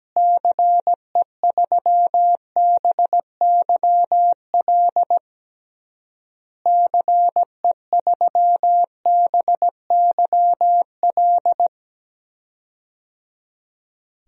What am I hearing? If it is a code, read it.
Morse code, heard as a keyed tone, about 17 wpm, 700 Hz. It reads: CE3BYL CE3BYL
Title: DXCC-100 Call Signs - 6 Characters - Repeated 3x - 17wpm